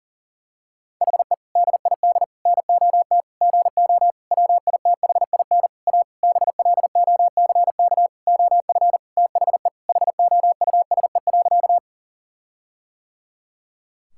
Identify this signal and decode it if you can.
Morse code, heard as a keyed tone, about 40 wpm, 700 Hz. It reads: HE DID NOT GO WITHIN A BLOCK OF THE HOUSE.
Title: Sentences from Top 1600 Words 40wpm